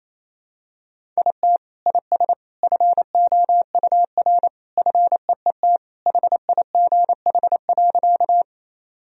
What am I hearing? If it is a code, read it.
Morse code, heard as a keyed tone, about 28 wpm, 700 Hz. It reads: IT IS FOUR FEET HIGH.
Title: Sentences from Top 1000 Words - Review Part 1 - 28wpm